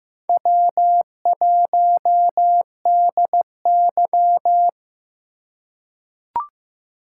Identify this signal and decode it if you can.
Morse code, heard as a keyed tone, about 15 wpm, 700 Hz. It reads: W1DY
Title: USA Call Signs - 4 Characters - 2x Character Spacing - 15wpm